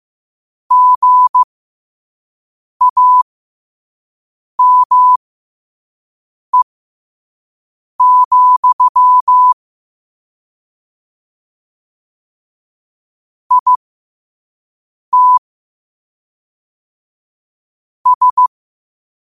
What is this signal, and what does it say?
Morse code, heard as a keyed tone, about 15 wpm, spaced as for 5 wpm, 1000 Hz. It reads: GAME, IT S
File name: qotdfiles / 5wpm